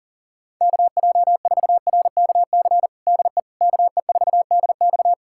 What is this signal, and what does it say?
Morse code, heard as a keyed tone, about 37 wpm, 700 Hz. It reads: KJ4RKC DE KE4DX
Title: USA Callsign DE Callsign 40wpm